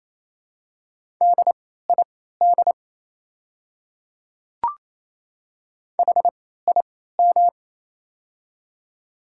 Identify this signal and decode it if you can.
Morse code, heard as a keyed tone, about 28 wpm, spaced as for 16 wpm, 700 Hz. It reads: DID HIM
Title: 3x Character Spacing - Top 100 Words - 28wpm